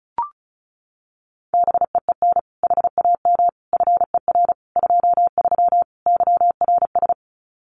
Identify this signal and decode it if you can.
Morse code, heard as a keyed tone, about 34 wpm, 700 Hz. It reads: BEEN HAM FER 23 YRS
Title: Common QSO Phrases Encore 35wpm